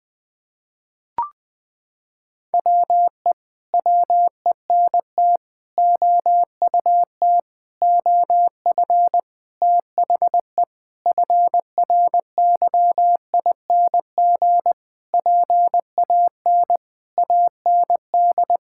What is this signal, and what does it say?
Morse code, heard as a keyed tone, about 20 wpm, 700 Hz. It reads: WE WENT OUT OF THE FRYING PAN AND
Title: Sentences with 9-Word Phrases - 20wpm